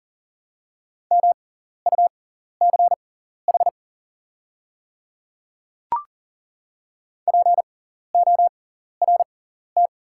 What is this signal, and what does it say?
Morse code, heard as a keyed tone, about 40 wpm, spaced as for 14 wpm, 700 Hz. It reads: MUCH PORT
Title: 6x Character Spacing - Top 200 Words - 40wpm